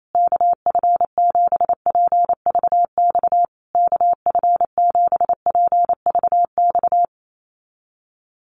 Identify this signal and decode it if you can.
Morse code, heard as a keyed tone, about 28 wpm, 700 Hz. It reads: KF7PVX KF7PVX
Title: USA Call Signs Repeated 3x 28wpm